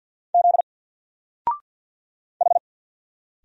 Morse code, heard as a keyed tone, about 48 wpm, 700 Hz. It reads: Z H
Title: Single Letter-Number 50wpm